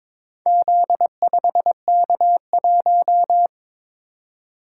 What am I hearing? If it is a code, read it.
Morse code, heard as a keyed tone, about 23 wpm, 700 Hz. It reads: Z5K1
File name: practice / 22wpm / Four Letter-Number Combinations - 22wpm